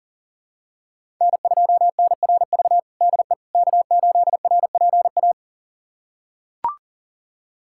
Morse code, heard as a keyed tone, about 40 wpm, 700 Hz. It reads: N2NRV DE K8RPA
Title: USA Callsign DE Callsign - 5 Characters - 40wpm